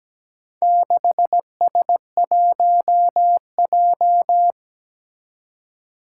Morse code, heard as a keyed tone, about 17 wpm, 700 Hz. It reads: 6S1J
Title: Four Challenging Letter-Number Combinations 17wpm